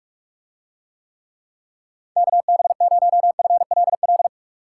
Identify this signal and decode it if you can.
Morse code, heard as a keyed tone, about 42 wpm, 700 Hz. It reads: KB0FLL